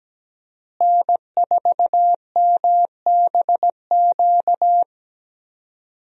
Morse code, heard as a keyed tone, about 17 wpm, 700 Hz. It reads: N4MBQ
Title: USA Call Signs - 5 Characters - Repeated 3x - 17wpm